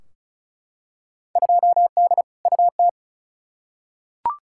Morse code, heard as a keyed tone, about 36 wpm, 700 Hz. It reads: 2D UT
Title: Field Day Exchanges Encore - 35wpm